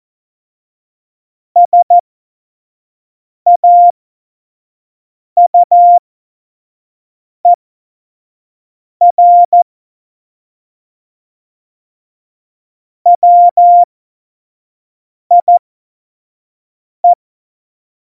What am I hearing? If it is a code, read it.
Morse code, heard as a keyed tone, about 14 wpm, spaced as for 5 wpm, 700 Hz. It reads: SAUER WIE
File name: Witz 31. Oktober 17, Tempo 25